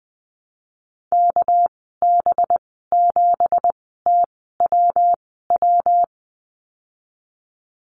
Morse code, heard as a keyed tone, about 20 wpm, spaced as for 15 wpm, 700 Hz. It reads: KB7TWW
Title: 238 Lesson 38g - Letter q - Call Signs 20wpm